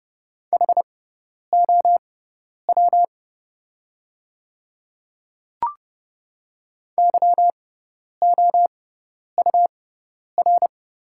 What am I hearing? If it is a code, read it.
Morse code, heard as a keyed tone, about 29 wpm, spaced as for 10 wpm, 700 Hz. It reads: HOW YOUR
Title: 6x Character Spacing - Top 100 Words - 30wpm